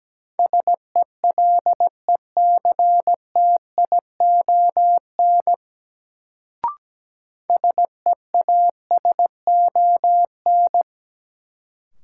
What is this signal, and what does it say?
Morse code, heard as a keyed tone, about 17 wpm, 700 Hz. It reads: SELECTION SEASON
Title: Sentences from Top 1400 Words - New Word Drill Alt - 17wpm